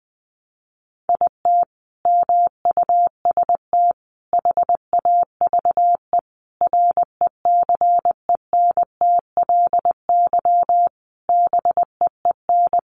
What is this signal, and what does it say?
Morse code, heard as a keyed tone, about 20 wpm, 700 Hz. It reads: IT MUST HAVE RECENTLY BEEN
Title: Sentences from Top 1200 Words 20wpm